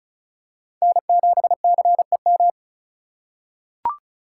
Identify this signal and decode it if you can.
Morse code, heard as a keyed tone, about 36 wpm, 700 Hz. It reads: N7CEM
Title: USA Call Signs - 5 Characters - 4x Character Spacing - 35wpm